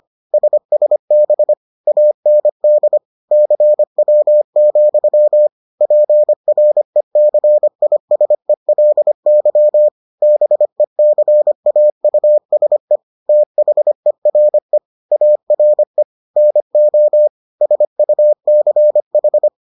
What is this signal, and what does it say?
Morse code, heard as a keyed tone, about 25 wpm, 600 Hz. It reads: SSB AND CW, PRECISELY BECAUSE THERE ARE NO SUCH